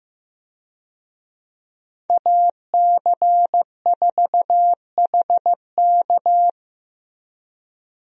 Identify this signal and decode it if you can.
Morse code, heard as a keyed tone, about 15 wpm, 700 Hz. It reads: AC4HK
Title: USA Call Signs - 5 Characters - Repeated 2x - 15wpm